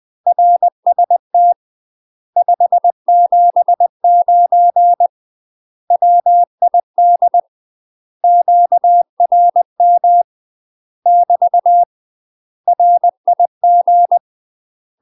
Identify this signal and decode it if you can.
Morse code, heard as a keyed tone, about 20 wpm, 700 Hz. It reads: RST 579 WID QRM <BT> RIG
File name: UK MASTER QSOs TRAINING / 20WPM / UK MASTER QSOs 0022